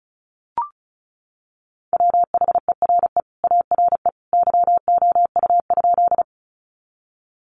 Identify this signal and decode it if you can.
Morse code, heard as a keyed tone, about 34 wpm, 700 Hz. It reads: WHERE ARE YOU?
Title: Sentences from Top 200 Words 35wpm